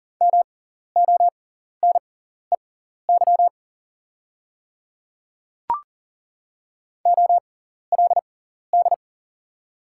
Morse code, heard as a keyed tone, about 38 wpm, spaced as for 14 wpm, 700 Hz. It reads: MONEY OLD
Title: 6x Character Spacing - Top 500 Words - 40wpm